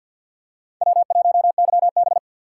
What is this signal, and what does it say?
Morse code, heard as a keyed tone, about 47 wpm, 700 Hz. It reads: W1YB
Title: USA Call Signs - 4 Characters - 3x Character Spacing - 50wpm